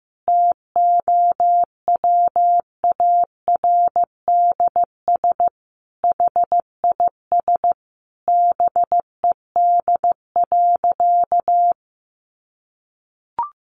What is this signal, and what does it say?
Morse code, heard as a keyed tone, about 15 wpm, 700 Hz. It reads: TOWARDS HIS BED.
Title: Sentences from Top 1100 Words 15wpm